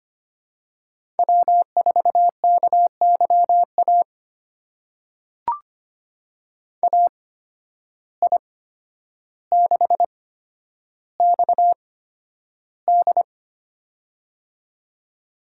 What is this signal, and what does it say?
Morse code, heard as a keyed tone, about 25 wpm, spaced as for 11 wpm, 700 Hz. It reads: W4KYA AI6XD
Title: USA Call Signs - 5 Characters - 8x Character Spacing - 25wpm